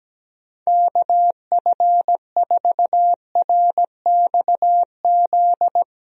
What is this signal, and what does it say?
Morse code, heard as a keyed tone, about 17 wpm, 700 Hz. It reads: KF4RXZ
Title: USA Call Signs - 6 Characters - Repeated 3x - 17wpm